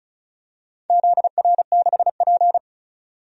Morse code, heard as a keyed tone, about 34 wpm, 700 Hz. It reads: ZR6P
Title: DXCC-100 Call Signs - 4 Characters - 35wpm